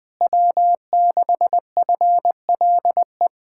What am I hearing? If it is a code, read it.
Morse code, heard as a keyed tone, about 20 wpm, 700 Hz. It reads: W6FLE
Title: Speed-Racing Common QSO Phrases - 20wpm